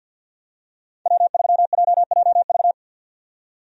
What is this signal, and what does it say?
Morse code, heard as a keyed tone, about 47 wpm, 700 Hz. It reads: W3JJV
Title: USA Call Signs - 5 Characters - 50wpm